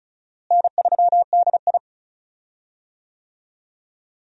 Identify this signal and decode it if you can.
Morse code, heard as a keyed tone, about 34 wpm, 700 Hz. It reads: N3DI
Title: USA Call Signs - 4 Characters - Repeated 3x - 35wpm